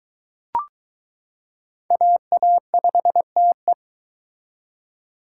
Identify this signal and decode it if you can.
Morse code, heard as a keyed tone, about 23 wpm, 700 Hz. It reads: AA5TE